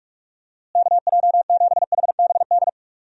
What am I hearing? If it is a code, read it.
Morse code, heard as a keyed tone, about 48 wpm, 700 Hz. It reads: KJ7HBD
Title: USA Call Signs - 6 Characters - Repeated 2x - 45wpm